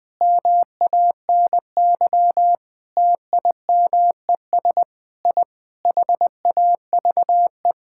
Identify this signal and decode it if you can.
Morse code, heard as a keyed tone, about 20 wpm, 700 Hz. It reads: MANY TIMES I HAVE
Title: Sentences with 7-Word Phrases - 20wpm